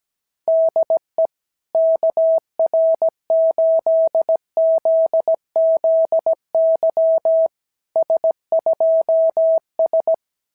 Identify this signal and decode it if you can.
Morse code, heard as a keyed tone, about 17 wpm, 660 Hz. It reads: DE KR8ZZY S2S
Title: SOTA QSO Practice 17wpm